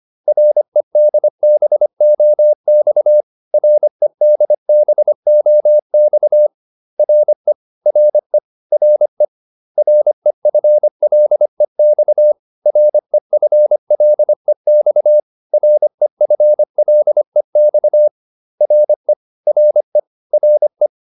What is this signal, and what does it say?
Morse code, heard as a keyed tone, about 25 wpm, 600 Hz. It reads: REDBOX REDBOX RE RE RE REFLEX REFLEX REFLEX RE RE RE